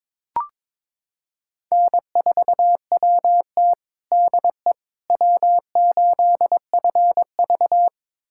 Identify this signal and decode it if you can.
Morse code, heard as a keyed tone, about 22 wpm, 700 Hz. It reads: N4WT DE W8FV
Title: USA Callsign DE Callsign - 4 Characters - 22wpm